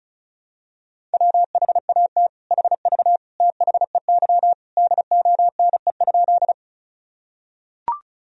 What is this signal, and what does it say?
Morse code, heard as a keyed tone, about 36 wpm, 700 Hz. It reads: WHAT HV THEY DONE?
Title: Sentences from Top 500 Words and Abbreviations 35wpm